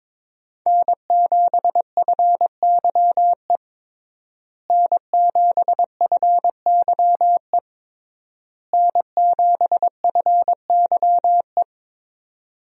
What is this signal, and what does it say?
Morse code, heard as a keyed tone, about 22 wpm, 700 Hz. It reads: N7FYE N7FYE N7FYE
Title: USA Call Signs - 5 Characters - Repeated 3x - 22wpm